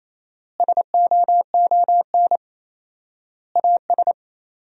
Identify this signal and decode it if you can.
Morse code, heard as a keyed tone, about 28 wpm, 700 Hz. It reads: SOON AS
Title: 4x Word-Spacing - Sentences from Top 500 Words - 28wpm